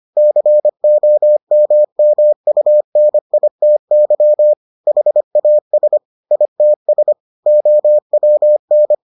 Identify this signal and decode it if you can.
Morse code, heard as a keyed tone, about 25 wpm, 600 Hz. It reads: COMMUNITY HAS ITS OWN